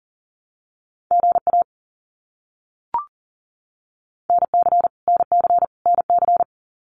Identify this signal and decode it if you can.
Morse code, heard as a keyed tone, about 40 wpm, 700 Hz. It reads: GA NC NC NC
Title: US States Repeated 3x 40wpm